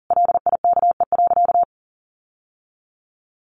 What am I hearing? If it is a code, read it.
Morse code, heard as a keyed tone, about 40 wpm, 700 Hz. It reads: LIKE.
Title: Sentences from Top 1300 Words 40wpm